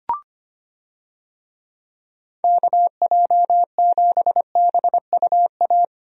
Morse code, heard as a keyed tone, about 25 wpm, 700 Hz. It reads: KJ7BUA
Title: USA Call Signs - 6 Characters - Repeated 3x - 25wpm